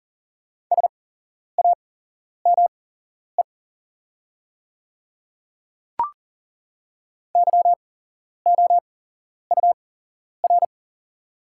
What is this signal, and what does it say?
Morse code, heard as a keyed tone, about 40 wpm, spaced as for 11 wpm, 700 Hz. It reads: SAME YOUR
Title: 8x Character Spacing - Top 200 Words - 40wpm